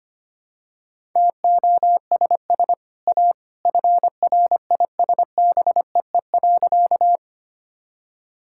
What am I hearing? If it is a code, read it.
Morse code, heard as a keyed tone, about 25 wpm, 700 Hz. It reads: TOSS A FRISBEE.
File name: Easy Sentences with Hints - 25wpm